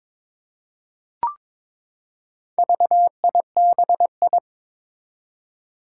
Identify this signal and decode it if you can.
Morse code, heard as a keyed tone, about 22 wpm, 700 Hz. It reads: VIBI
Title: Four Challenging Letter-Number Combinations 22wpm